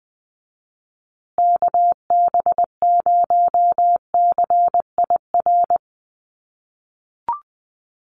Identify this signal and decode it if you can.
Morse code, heard as a keyed tone, about 20 wpm, 700 Hz. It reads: KB0CIR
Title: USA Call Signs Repeated 3x 20wpm